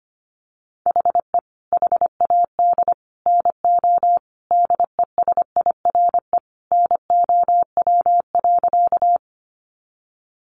Morse code, heard as a keyed tone, about 25 wpm, 690 Hz. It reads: HE HAD NO DESIRE NOW.
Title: Sentences from Top 1100 Words 25wpm